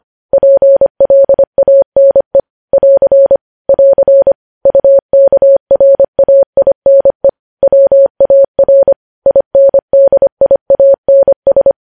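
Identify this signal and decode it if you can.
Morse code, heard as a keyed tone, about 25 wpm, 550 Hz. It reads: PLANE <AR> <AR> UKRAINE WAR INDIANS
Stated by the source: News headlines sent at 25.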